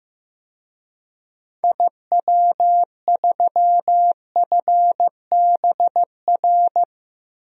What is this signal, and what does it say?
Morse code, heard as a keyed tone, about 15 wpm, 700 Hz. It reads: IW3FBR
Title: DXCC-100 Call Signs - 6 Characters - 15wpm